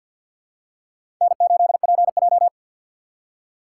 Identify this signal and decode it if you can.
Morse code, heard as a keyed tone, about 52 wpm, 700 Hz. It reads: N8PJ
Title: USA Call Signs - 4 Characters - 2x Character Spacing - 50wpm